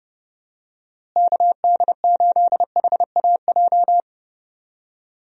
Morse code, heard as a keyed tone, about 32 wpm, 700 Hz. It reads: KD8HAJ
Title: USA Call Signs - 6 Characters - Repeated 3x - 30wpm